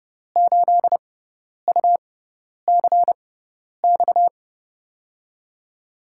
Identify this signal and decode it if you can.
Morse code, heard as a keyed tone, about 28 wpm, spaced as for 10 wpm, 700 Hz. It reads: 8UCX
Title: USA Call Signs - 6x Character Spacing - 30wpm